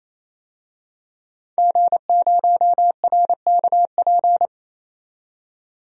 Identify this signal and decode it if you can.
Morse code, heard as a keyed tone, about 29 wpm, 690 Hz. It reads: G0RKP
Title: DXCC-100 Call Signs - 5 Characters - 28wpm